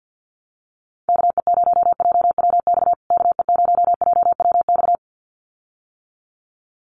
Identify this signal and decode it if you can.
Morse code, heard as a keyed tone, about 50 wpm, 700 Hz. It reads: KE0JWX KE0JWX
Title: USA Call Signs Repeated 2x 50wpm